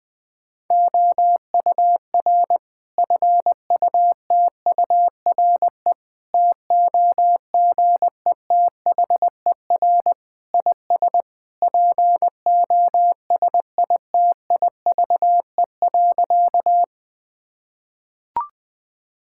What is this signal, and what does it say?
Morse code, heard as a keyed tone, about 21 wpm, 700 Hz. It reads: OUR FUTURE TOGETHER IS POSITIVE.
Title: Sentences from Top 1000 Words - Review Part 5 - 20wpm